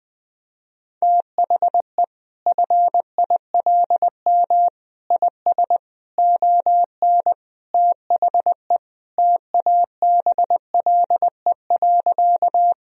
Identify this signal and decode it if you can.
Morse code, heard as a keyed tone, about 20 wpm, 700 Hz. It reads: THE FILM IS ON THE TABLE.
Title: Sentences from Top 1000 Words - Review Part 3 - 20wpm